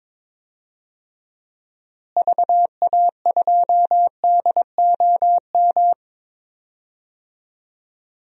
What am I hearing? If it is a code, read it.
Morse code, heard as a keyed tone, about 22 wpm, 700 Hz. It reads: VA2DOM
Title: DXCC-100 Call Signs - 6 Characters - Repeated 3x - 22wpm